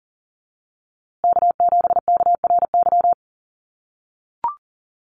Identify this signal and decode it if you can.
Morse code, heard as a keyed tone, about 40 wpm, 700 Hz. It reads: K7KRY
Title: USA Call Signs Repeated 2x 40wpm